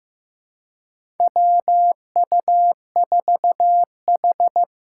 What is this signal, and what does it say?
Morse code, heard as a keyed tone, about 15 wpm, 700 Hz. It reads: WU4H
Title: USA Call Signs - 4 Characters - 3x Character Spacing - 15wpm